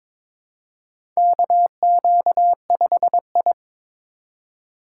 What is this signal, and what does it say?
Morse code, heard as a keyed tone, about 22 wpm, 700 Hz. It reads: KQ5I
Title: USA Call Signs - 4 Characters - 2x Character Spacing - 22wpm